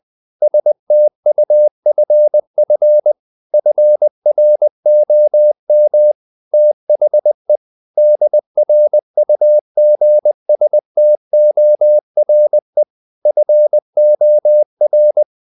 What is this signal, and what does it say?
Morse code, heard as a keyed tone, about 20 wpm, 600 Hz. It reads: STUFF FROM THE DRUGSTORE FOR
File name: the old man and the sea part 2 20wpm